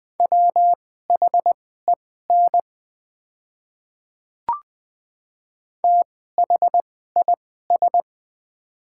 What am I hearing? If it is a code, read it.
Morse code, heard as a keyed tone, about 20 wpm, spaced as for 14 wpm, 700 Hz. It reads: WHEN THIS
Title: 2x Character Spacing - Top 100 Words - 20wpm